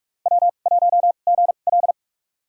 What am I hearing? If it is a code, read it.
Morse code, heard as a keyed tone, about 43 wpm, spaced as for 32 wpm, 700 Hz. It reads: W1GL
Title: USA Call Signs - 4 Characters - 2x Character Spacing - 45wpm